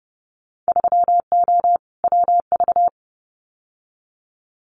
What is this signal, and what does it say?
Morse code, heard as a keyed tone, about 30 wpm, 700 Hz. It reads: 3O WV
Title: Winter Field Day Exchanges 30wpm